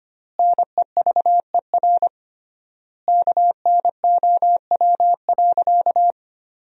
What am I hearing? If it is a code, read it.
Morse code, heard as a keyed tone, about 25 wpm, 700 Hz. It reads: NEVER KNOW.
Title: 3x Word-Spacing - Sentences from Top 300 Words - 25wpm